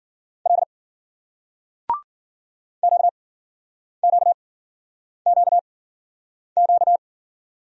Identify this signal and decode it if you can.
Morse code, heard as a keyed tone, about 53 wpm, spaced as for 9 wpm, 700 Hz. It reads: L QQQQ